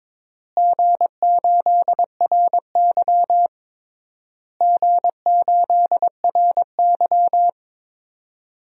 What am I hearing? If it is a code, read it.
Morse code, heard as a keyed tone, about 22 wpm, 700 Hz. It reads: G8RY G8RY
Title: DXCC-100 Call Signs - 4 Characters - Repeated 3x - 22wpm